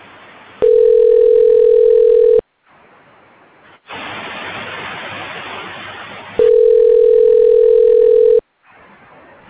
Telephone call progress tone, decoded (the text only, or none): ringback tone